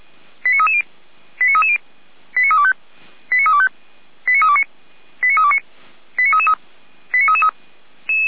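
Five-tone selective call (ZVEI), heard as quick runs of five tones, tones of about 70 ms. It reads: 89330 89330 89325 89325 89329 89329 89333 89333
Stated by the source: ZVEI five-tone selective call received on an SDR, from the audio archive of Feuerwehr Orlamünde.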